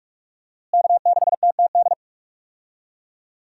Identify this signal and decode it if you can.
Morse code, heard as a keyed tone, about 47 wpm, 700 Hz. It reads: K6TTD
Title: USA Call Signs - 5 Characters - Repeated 2x - 45wpm